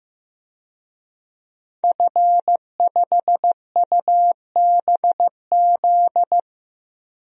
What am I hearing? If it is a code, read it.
Morse code, heard as a keyed tone, about 16 wpm, 700 Hz. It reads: F5UBZ
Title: DXCC-100 Call Signs - 5 Characters - Repeated 3x - 15wpm